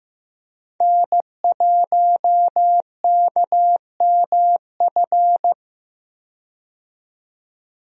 Morse code, heard as a keyed tone, about 15 wpm, 700 Hz. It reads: N1KMF
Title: USA Call Signs - 5 Characters - 15wpm